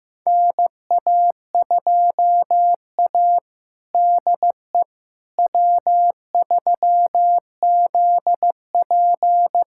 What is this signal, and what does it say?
Morse code, heard as a keyed tone, about 15 wpm, 700 Hz. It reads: NA2A DE W3ZP
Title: USA Callsign DE Callsign - 4 Characters - 15wpm